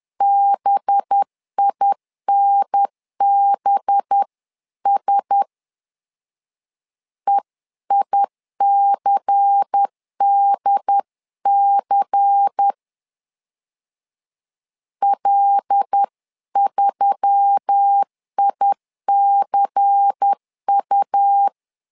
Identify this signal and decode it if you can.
Morse code, heard as a keyed tone, about 10 wpm, 800 Hz. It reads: BINBS EICDC L3ICU
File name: LEZIONI / 5 LBVC / 5 LBVC 10wpm 2349